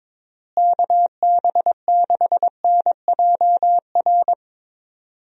Morse code, heard as a keyed tone, about 22 wpm, 700 Hz. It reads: KB6NJR
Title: USA Call Signs - 6 Characters - Repeated 3x - 22wpm